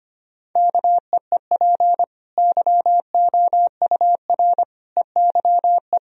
Morse code, heard as a keyed tone, about 25 wpm, 700 Hz. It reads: KEEP YOUR EYE
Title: Sentences from Top 1000 Words - Review Part 1 - 25wpm